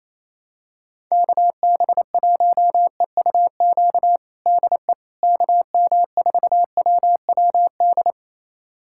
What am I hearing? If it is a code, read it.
Morse code, heard as a keyed tone, about 28 wpm, 700 Hz. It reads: KB1EUQ DE KM4WWD
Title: USA Callsign DE Callsign - 6 Characters - 28wpm